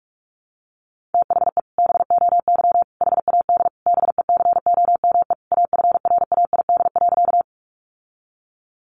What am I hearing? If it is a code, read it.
Morse code, heard as a keyed tone, about 44 wpm, 700 Hz. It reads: THE BOY HAD BECOME AFRAID.
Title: Sentences from Top 1500 Words 45wpm